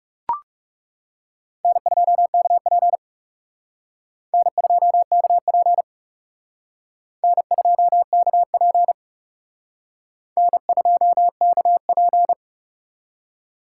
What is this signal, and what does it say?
Morse code, heard as a keyed tone, about 37 wpm, 700 Hz. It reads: N2KP N2KP N2KP N2KP
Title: Speed Racing - USA 1x2 Call Signs 30wpm